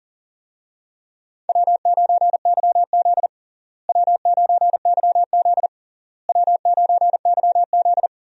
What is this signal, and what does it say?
Morse code, heard as a keyed tone, about 40 wpm, 700 Hz. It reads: W9YZ W9YZ W9YZ
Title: USA Call Signs - 4 Characters - Repeated 3x - 40wpm